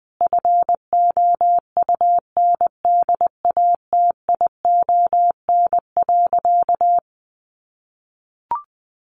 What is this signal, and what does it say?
Morse code, heard as a keyed tone, about 20 wpm, 700 Hz. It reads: FOUNDATION.
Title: Sentences from Top 1200 Words 20wpm